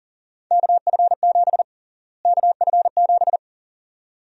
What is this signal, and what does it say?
Morse code, heard as a keyed tone, about 38 wpm, 700 Hz. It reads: KF7 KF7
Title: USA Call Sign Prefixes - Repeated 2x - 40wpm